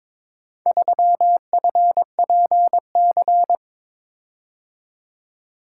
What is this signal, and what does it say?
Morse code, heard as a keyed tone, about 22 wpm, 700 Hz. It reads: 3FPC
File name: practice / 22wpm / Four Letter-Number Combinations - 22wpm